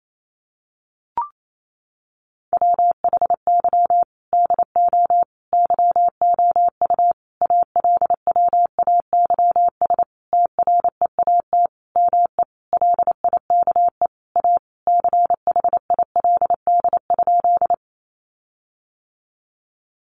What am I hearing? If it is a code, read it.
Morse code, heard as a keyed tone, about 28 wpm, 700 Hz. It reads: WHY DO YOU ALWAYS TREAT ME LIKE A CHILD?
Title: Sentences from Top 1300 Words 28wpm